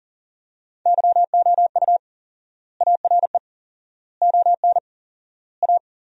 Morse code, heard as a keyed tone, about 40 wpm, 700 Hz. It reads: YOU ARE ON A
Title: 4x Word-Spacing - Sentences from Top 400 Words - 40wpm